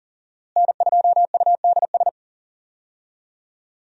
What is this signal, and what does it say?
Morse code, heard as a keyed tone, about 40 wpm, 700 Hz. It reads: N2UDS
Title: USA Call Signs - 5 Characters - Repeated 2x - 40wpm